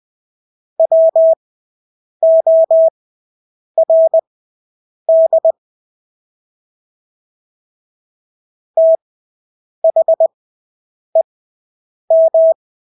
Morse code, heard as a keyed tone, about 20 wpm, spaced as for 8 wpm, 650 Hz. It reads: WORD THEM
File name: Engelske ord 20/8 wpm